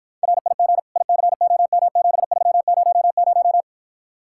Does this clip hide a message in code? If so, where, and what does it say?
Yes: Morse code from 0.2 to 3.6 s, heard as a keyed tone, about 52 wpm, 700 Hz. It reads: RIG ICOM7300